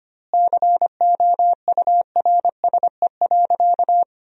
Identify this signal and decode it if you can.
Morse code, heard as a keyed tone, about 25 wpm, 700 Hz. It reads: COURSE.
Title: 2x Word-Spacing - Sentences from Top 500 Words - 25wpm